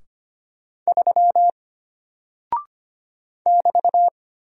Morse code, heard as a keyed tone, about 25 wpm, 700 Hz. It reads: 3 <BT>
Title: Single Character (Letter:Number:Punctuation:Prosign) Alt 25wpm